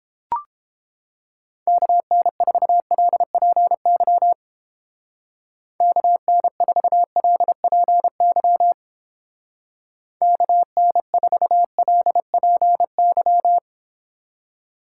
Morse code, heard as a keyed tone, about 29 wpm, 690 Hz. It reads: KN4LPY KN4LPY KN4LPY